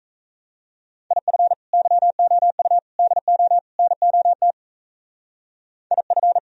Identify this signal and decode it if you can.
Morse code, heard as a keyed tone, about 34 wpm, 700 Hz. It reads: IF YOU DO NOT IF
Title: Speed-Racing - Sets of 4 Words 28wpm